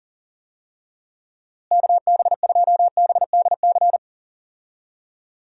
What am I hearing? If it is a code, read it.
Morse code, heard as a keyed tone, about 39 wpm, 700 Hz. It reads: KB2BDC
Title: USA Call Signs - 6 Characters - Repeated 2x - 40wpm